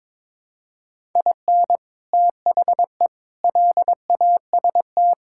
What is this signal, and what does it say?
Morse code, heard as a keyed tone, about 22 wpm, 700 Hz. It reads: IN THE LAST
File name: practice / 22wpm / Build A Phrase - 5-Word Phrases - 22wpm